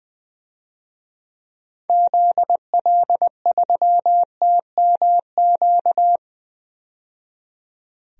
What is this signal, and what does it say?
Morse code, heard as a keyed tone, about 20 wpm, 700 Hz. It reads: ZL3TMQ
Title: DXCC-100 Call Signs - 6 Characters - 20wpm